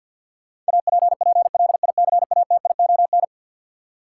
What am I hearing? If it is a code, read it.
Morse code, heard as a keyed tone, about 51 wpm, 700 Hz. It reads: APPLICATION
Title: Sentences from Top 1200 Words - New Word Drill - 50wpm